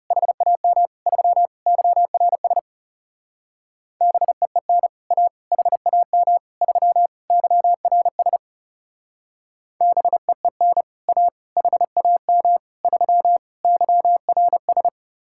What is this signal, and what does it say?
Morse code, heard as a keyed tone, about 37 wpm, 700 Hz. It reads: HAM 3 YRS BEEN A HAM 3 YRS BEEN A HAM 3 YRS
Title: Speed-Racing Common QSO Phrases Encore - 30wpm